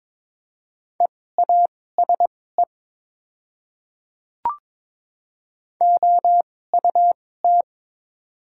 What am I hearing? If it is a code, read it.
Morse code, heard as a keyed tone, about 21 wpm, spaced as for 16 wpm, 700 Hz. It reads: EASE OUT
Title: 2x Character Spacing - Top 500 Words - 22wpm